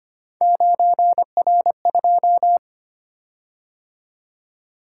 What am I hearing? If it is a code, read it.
Morse code, heard as a keyed tone, about 25 wpm, 700 Hz. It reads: 9R2
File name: Three Letter-Number Combinations - 25wpm